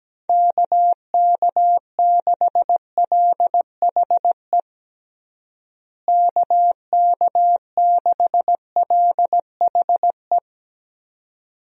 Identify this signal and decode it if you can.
Morse code, heard as a keyed tone, about 17 wpm, 700 Hz. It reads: KK6LHE KK6LHE
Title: USA Call Signs - 6 Characters - Repeated 2x - 17wpm